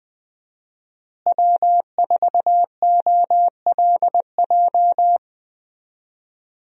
Morse code, heard as a keyed tone, about 21 wpm, 700 Hz. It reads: W4OLJ
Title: USA Call Signs - 5 Characters - Repeated 2x - 20wpm